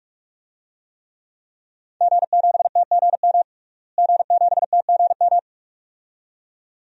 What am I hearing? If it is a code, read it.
Morse code, heard as a keyed tone, about 46 wpm, 700 Hz. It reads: G7TGM G7TGM
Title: DXCC-100 Call Signs - 5 Characters - Repeated 2x - 45wpm